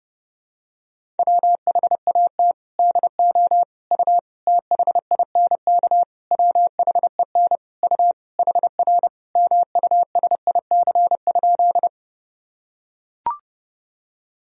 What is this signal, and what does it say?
Morse code, heard as a keyed tone, about 30 wpm, 700 Hz. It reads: WHAT DO U THINK WHEN U HR MUSIC?
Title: Sentences from Top 300 Words and Abbreviations 30wpm